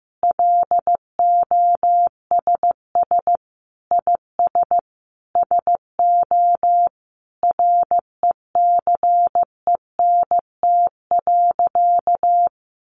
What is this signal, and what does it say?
Morse code, heard as a keyed tone, about 15 wpm, 700 Hz. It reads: LOSS IS SO RECENT.
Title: Sentences from Top 1400 Words 15wpm